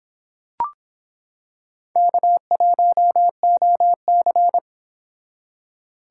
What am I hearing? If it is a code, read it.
Morse code, heard as a keyed tone, about 25 wpm, 700 Hz. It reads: K1OC